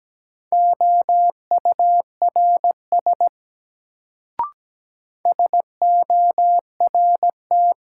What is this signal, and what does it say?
Morse code, heard as a keyed tone, about 17 wpm, 700 Hz. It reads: OURS SORT
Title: Sentences from Top 1300 Words - New Word Drill Alt - 17wpm